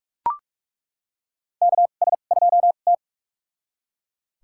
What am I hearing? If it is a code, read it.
Morse code, heard as a keyed tone, about 45 wpm, spaced as for 33 wpm, 700 Hz. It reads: KS2T
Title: USA Call Signs - 4 Characters - 2x Character Spacing - 45wpm